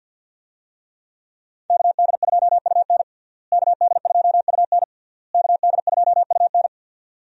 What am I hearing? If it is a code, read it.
Morse code, heard as a keyed tone, about 52 wpm, 700 Hz. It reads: KD2UN KD2UN KD2UN
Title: USA Call Signs - 5 Characters - Repeated 3x - 50wpm